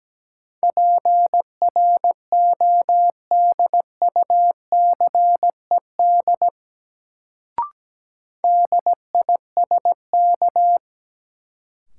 Morse code, heard as a keyed tone, about 17 wpm, 700 Hz. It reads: PRODUCED DISK
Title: Sentences from Top 1000 Words - Review Part 6 - New Word Drill - 17wpm